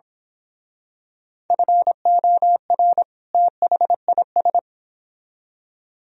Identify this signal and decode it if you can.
Morse code, heard as a keyed tone, about 26 wpm, 700 Hz. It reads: FOR THIS
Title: Speed-Racing - Sets of 2 Words 22wpm